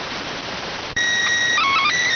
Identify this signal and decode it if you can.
SSTV (slow-scan television), PD 290.
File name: findings / MR CORONAL 55.gen-11 rx